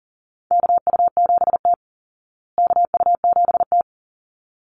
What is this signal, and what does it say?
Morse code, heard as a keyed tone, about 40 wpm, 700 Hz. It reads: KU7T KU7T
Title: Call Signs Repeated 3x - 40wpm